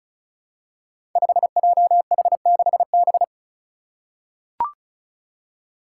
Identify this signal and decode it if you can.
Morse code, heard as a keyed tone, about 36 wpm, 700 Hz. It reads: 5JH6B
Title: Five Challenging Letter-Number Combinations 35wpm